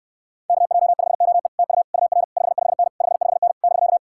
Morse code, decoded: KJ5OE TU UR 55N 55N <BK>

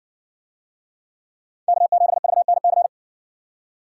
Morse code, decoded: K7VNC

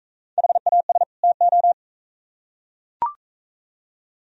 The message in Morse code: HAS TO

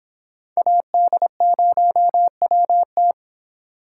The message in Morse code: AD0WT

